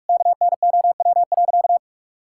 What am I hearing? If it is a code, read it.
Morse code, heard as a keyed tone, about 46 wpm, 700 Hz. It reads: KNOW.
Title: Sentences from Top 1600 Words 45wpm